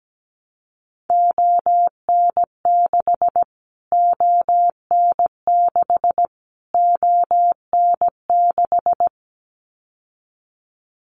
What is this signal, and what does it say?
Morse code, heard as a keyed tone, about 17 wpm, 700 Hz. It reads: ON6 ON6 ON6
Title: Callsign Prefixes for Top 25 DXCC - Repeated 3x - 17wpm